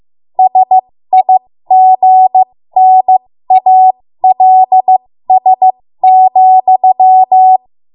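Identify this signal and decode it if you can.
Morse code, heard as a keyed tone, about 15 wpm, spaced as for 13 wpm, 750 Hz. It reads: SIGNALS,
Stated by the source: ARRL Morse code practice at 13 WPM.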